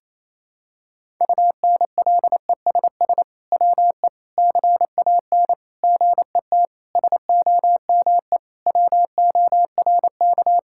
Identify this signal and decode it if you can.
Morse code, heard as a keyed tone, about 27 wpm, 700 Hz. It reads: UNLESS WE CAN GET SOME WORK
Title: Sentences from Top 1000 Words - Review Part 6 - 28wpm